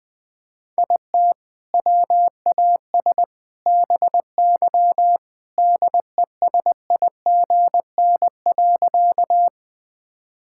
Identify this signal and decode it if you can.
Morse code, heard as a keyed tone, about 21 wpm, 700 Hz. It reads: IT WAS BY DESIGN.
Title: Sentences from Top 1000 Words - Review Part 5 - 20wpm